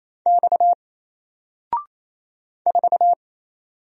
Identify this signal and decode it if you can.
Morse code, heard as a keyed tone, about 27 wpm, 700 Hz. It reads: X 4
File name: Single-Letter-Number-28wpm